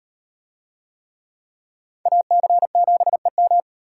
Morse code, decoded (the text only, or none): AC7EM